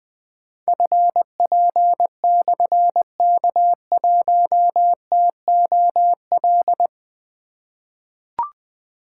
Morse code, heard as a keyed tone, about 20 wpm, 700 Hz. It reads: FP/K1TOL
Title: Difficult Call Signs 20wpm